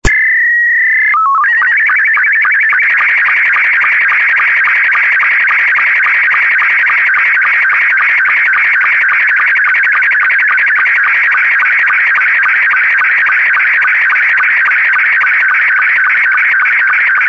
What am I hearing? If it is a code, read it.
SSTV (slow-scan television), Scottie 2.
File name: sstv / sstv2 / scottie s2